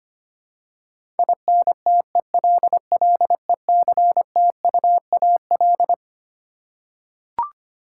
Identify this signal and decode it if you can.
Morse code, heard as a keyed tone, about 25 wpm, 700 Hz. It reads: INTELLECTUAL
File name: Twelve Letter Words - 25wpm